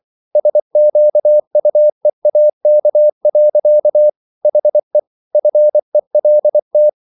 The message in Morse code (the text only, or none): SQUEAK. HE FELT